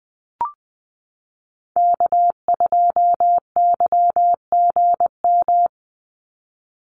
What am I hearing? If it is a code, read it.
Morse code, heard as a keyed tone, about 20 wpm, 700 Hz. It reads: K2YGM
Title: Call Signs Repeated 3x - 20wpm